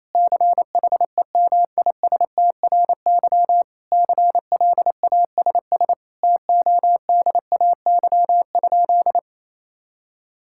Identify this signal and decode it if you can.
Morse code, heard as a keyed tone, about 28 wpm, 700 Hz. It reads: CHEMISTRY CLASS TODAY?
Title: Sentences from Top 1000 Words - Review Part 6 - 28wpm